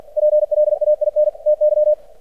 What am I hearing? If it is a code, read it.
Morse code, heard as a keyed tone, about 47 wpm, 600 Hz. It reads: PLAIN TY
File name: archived / CRSnet